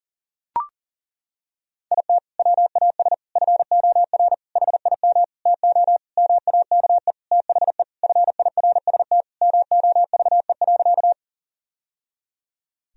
Morse code, heard as a keyed tone, about 37 wpm, 700 Hz. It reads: IT WAS FOR HIM TO MAKE THE FIRST MOVE.
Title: Sentences from Top 1000 Words - Review Part 1 - 40wpm